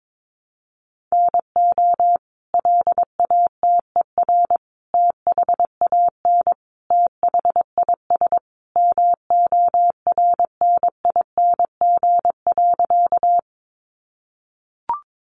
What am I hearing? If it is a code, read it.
Morse code, heard as a keyed tone, about 22 wpm, 700 Hz. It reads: NO LATER THAN THIS MORNING.
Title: Sentences from Top 700 Words 22wpm